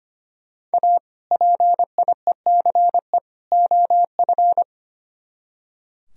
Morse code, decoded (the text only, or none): A PIECE OF